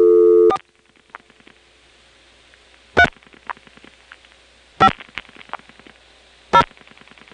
DTMF, heard as telephone touch tones, tones of about 70 ms. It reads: 4 3 5 0
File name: phonetrips / SmyrnaGA5HQ